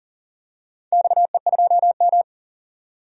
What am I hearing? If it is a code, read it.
Morse code, heard as a keyed tone, about 40 wpm, 700 Hz. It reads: XE2M
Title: DXCC-100 Call Signs - 4 Characters - 40wpm